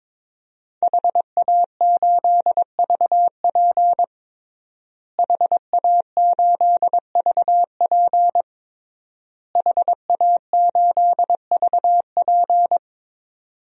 Morse code, heard as a keyed tone, about 22 wpm, 700 Hz. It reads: HA8VP HA8VP HA8VP
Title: DXCC-100 Call Signs - 5 Characters - Repeated 3x - 22wpm